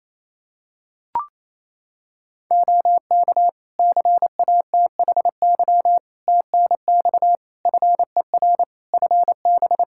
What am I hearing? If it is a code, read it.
Morse code, heard as a keyed tone, about 27 wpm, 700 Hz. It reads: OK CATHY TNX FER FB
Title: Practice QSO - Exchange 4 - Ending - 28wpm